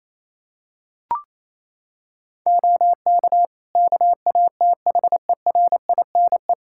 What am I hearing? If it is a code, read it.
Morse code, heard as a keyed tone, about 27 wpm, 700 Hz. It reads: OK KATHERINE
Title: Practice QSO - Exchange 3 - More Info - 28wpm